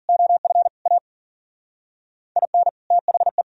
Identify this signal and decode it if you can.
Morse code, heard as a keyed tone, about 55 wpm, 700 Hz. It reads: OF A IN THE